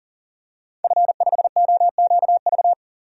Morse code, heard as a keyed tone, about 40 wpm, 700 Hz. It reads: F5OQV
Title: DXCC-100 Call Signs - 5 Characters - 40wpm